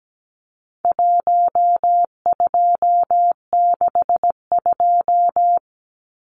Morse code, heard as a keyed tone, about 17 wpm, 700 Hz. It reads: 1262
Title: Numbers - 4-Digit 17wpm